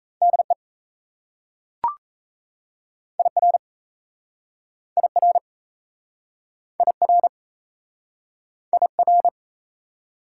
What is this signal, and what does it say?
Morse code, heard as a keyed tone, about 38 wpm, 700 Hz. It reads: DE IR IR IR IR